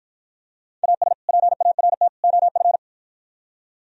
Morse code, heard as a keyed tone, about 52 wpm, 700 Hz. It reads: AS PART OF